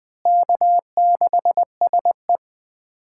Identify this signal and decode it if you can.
Morse code, heard as a keyed tone, about 21 wpm, 700 Hz. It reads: K6SE
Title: USA Callsign DE Callsign - 4 Characters - 20wpm